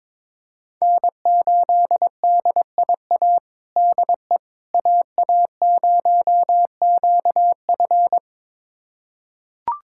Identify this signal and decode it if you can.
Morse code, heard as a keyed tone, about 22 wpm, 700 Hz. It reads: N8DIA DE AA0QF
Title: USA Callsign DE Callsign - 5 Characters - 22wpm